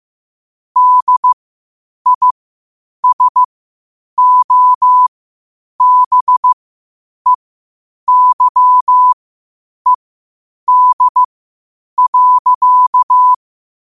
Morse code, heard as a keyed tone, about 15 wpm, spaced as for 8 wpm, 1000 Hz. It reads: DISOBEYED.
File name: qotdfiles / 8wpm